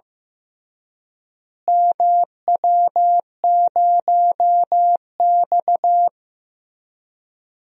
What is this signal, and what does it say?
Morse code, heard as a keyed tone, about 15 wpm, 700 Hz. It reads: MW0X
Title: DXCC-100 Call Signs - 4 Characters - Repeated 2x - 15wpm